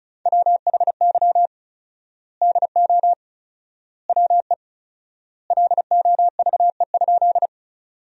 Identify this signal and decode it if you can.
Morse code, heard as a keyed tone, about 34 wpm, 700 Hz. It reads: WHY DO WE LOVE?
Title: 4x Word-Spacing - Sentences from Top 400 Words - 35wpm